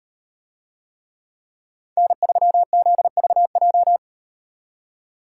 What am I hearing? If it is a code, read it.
Morse code, heard as a keyed tone, about 38 wpm, 700 Hz. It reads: N3ZVJ